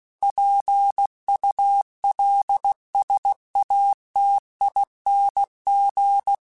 Morse code, heard as a keyed tone, about 16 wpm, 800 Hz. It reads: PULSATING
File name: Technical Subjects / Technical Subjects-mp3s / Sw Pwr Sup @15